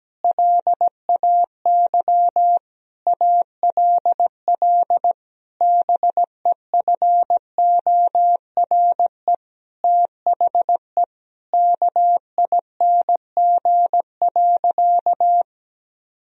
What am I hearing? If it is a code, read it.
Morse code, heard as a keyed tone, about 17 wpm, 700 Hz. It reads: LAY ALL BEFORE THE KING.
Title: Sentences from Top 1000 Words - Review Part 2 - 17wpm